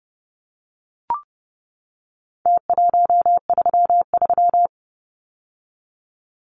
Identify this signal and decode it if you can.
Morse code, heard as a keyed tone, about 30 wpm, 700 Hz. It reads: T133
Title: Numbers - Contest Style - 4-Digit 30wpm